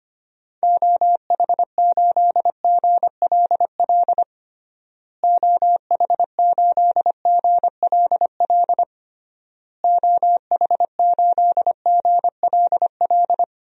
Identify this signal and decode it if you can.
Morse code, heard as a keyed tone, about 25 wpm, 700 Hz. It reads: OH8GLL OH8GLL OH8GLL
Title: DXCC-100 Call Signs - 6 Characters - Repeated 3x - 25wpm